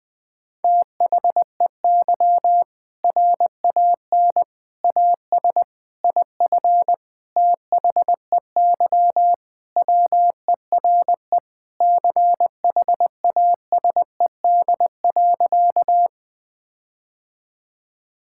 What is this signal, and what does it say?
Morse code, heard as a keyed tone, about 21 wpm, 700 Hz. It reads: THEY RAN AS IF THEY WERE CHASED.
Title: Sentences with 4-Word Phrases - 20wpm